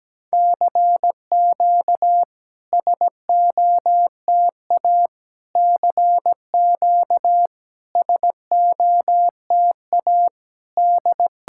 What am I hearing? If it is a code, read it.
Morse code, heard as a keyed tone, about 17 wpm, 700 Hz. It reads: CQ SOTA CQ SOTA D
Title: SOTA QSO Practice 17wpm